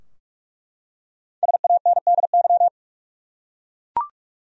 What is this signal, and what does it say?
Morse code, heard as a keyed tone, about 46 wpm, 700 Hz. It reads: SANDY
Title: USA Common Names - Max 5 Letters - 45wpm